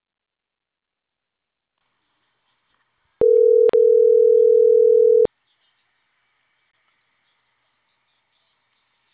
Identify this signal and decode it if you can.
Ringback tone (North America), 440 + 480 Hz.